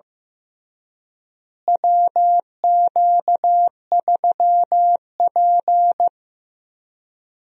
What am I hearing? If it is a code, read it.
Morse code, heard as a keyed tone, about 15 wpm, 700 Hz. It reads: WQ3P
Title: USA Call Signs - 4 Characters - Repeated 3x - 15wpm